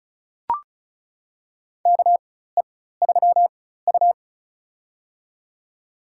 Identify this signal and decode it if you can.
Morse code, heard as a keyed tone, about 34 wpm, spaced as for 16 wpm, 700 Hz. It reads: KE3U
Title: USA Call Signs - 4 Characters - 4x Character Spacing - 35wpm